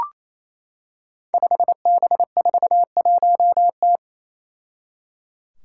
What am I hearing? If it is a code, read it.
Morse code, heard as a keyed tone, about 28 wpm, 700 Hz. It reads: 5B41T
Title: Five Challenging Letter-Number Combinations 28wpm